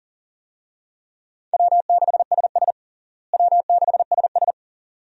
Morse code, heard as a keyed tone, about 37 wpm, 700 Hz. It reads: W6SS W6SS
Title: USA Call Signs - 4 Characters - Repeated 2x - 40wpm